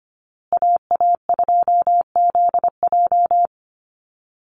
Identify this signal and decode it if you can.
Morse code, heard as a keyed tone, about 25 wpm, 700 Hz. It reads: AA2ZJ
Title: USA Call Signs Repeated 2x 25wpm